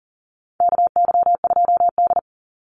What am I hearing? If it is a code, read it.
Morse code, heard as a keyed tone, about 40 wpm, 700 Hz. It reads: KY2D
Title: Call Signs - 40wpm